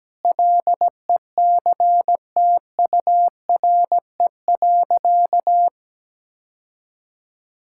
Morse code, heard as a keyed tone, about 17 wpm, 700 Hz. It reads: LECTURE.
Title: Sentences from Top 1600 Words 17wpm